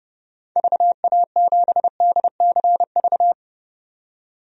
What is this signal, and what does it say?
Morse code, heard as a keyed tone, about 30 wpm, 700 Hz. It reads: VA7DCV
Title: DXCC-100 Call Signs - 6 Characters - 30wpm